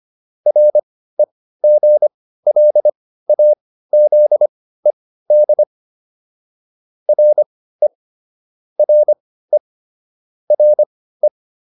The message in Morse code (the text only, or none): REGLAZED RE RE RE